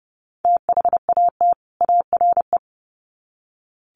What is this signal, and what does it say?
Morse code, heard as a keyed tone, about 30 wpm, 700 Hz. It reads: THAT ARE
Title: Sets of 2 Words 30wpm